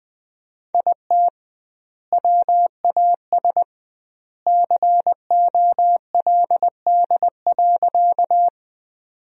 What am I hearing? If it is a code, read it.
Morse code, heard as a keyed tone, about 20 wpm, 700 Hz. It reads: IT WAS COLD.
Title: 2x Word-Spacing - Sentences from Top 400 Words - 20wpm